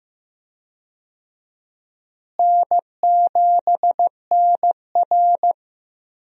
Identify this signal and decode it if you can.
Morse code, heard as a keyed tone, about 15 wpm, 700 Hz. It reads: N7NR
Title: USA Call Signs - 4 Characters - Repeated 2x - 15wpm